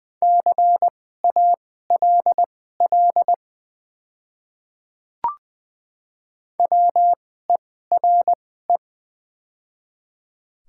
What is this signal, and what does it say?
Morse code, heard as a keyed tone, about 20 wpm, spaced as for 14 wpm, 700 Hz. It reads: CALL WERE
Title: 2x Character Spacing - Top 500 Words - 20wpm